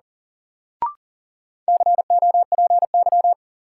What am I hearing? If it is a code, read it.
Morse code, heard as a keyed tone, about 37 wpm, 700 Hz. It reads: COPY
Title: Sentences from Top 1400 Words - New Word Drill - 40wpm